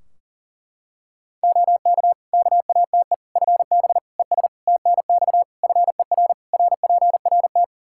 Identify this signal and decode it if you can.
Morse code, heard as a keyed tone, about 40 wpm, 700 Hz. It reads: OK KATE FB ES TNX FER RPRT
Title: Practice QSO - Exchange 2 - Info - 40wpm